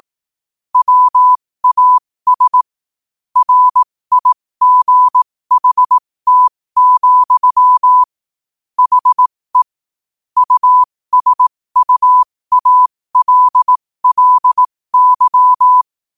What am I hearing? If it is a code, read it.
Morse code, heard as a keyed tone, about 18 wpm, spaced as for 15 wpm, 1000 Hz. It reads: WAS RIGHT, HE USUALLY
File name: qotdfiles / 15wpm